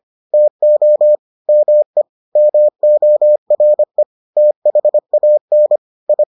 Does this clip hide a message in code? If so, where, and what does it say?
Yes: Morse code from 0.3 to 6.2 s, heard as a keyed tone, about 25 wpm, 600 Hz. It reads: TO ME MORE THAN I